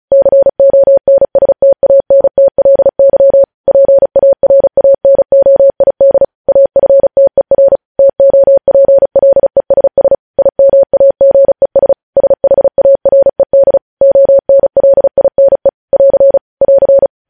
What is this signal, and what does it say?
Morse code, heard as a keyed tone, about 36 wpm, 550 Hz. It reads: CONSTANTLY PARANOID AFTER TOPLESS IMAGES SHARED ONLINE <AR> <AR>
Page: news headlines in morse code at 35 wpm